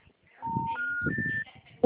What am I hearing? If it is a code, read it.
Special information tone, with speech.
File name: out- -2001- - - .66352